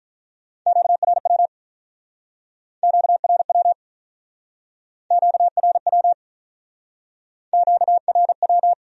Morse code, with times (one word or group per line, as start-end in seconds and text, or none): 0.7-1.5 QRW
2.8-3.7 QRW
5.1-6.1 QRW
7.5-8.7 QRW